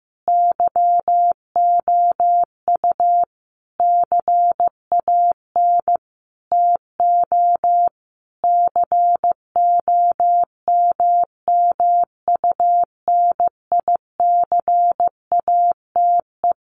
Morse code, heard as a keyed tone, about 15 wpm, 700 Hz. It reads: YOU CAN TO COMMUNICATE
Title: Sentences from Top 1300 Words 15wpm